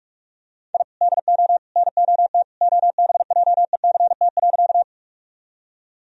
Morse code, heard as a keyed tone, about 46 wpm, 700 Hz. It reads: I DO NOT OBJECT.
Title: Sentences from Top 1000 Words - Review Part 3 - 45wpm